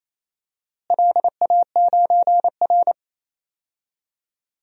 Morse code, heard as a keyed tone, about 28 wpm, 700 Hz. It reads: LA9R
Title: DXCC-100 Call Signs - 4 Characters - Repeated 2x - 28wpm